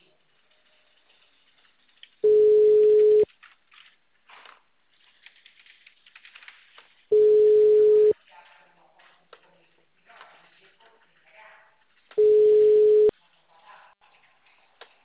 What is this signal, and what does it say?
Ringback tone (Europe, 425 Hz), with speech.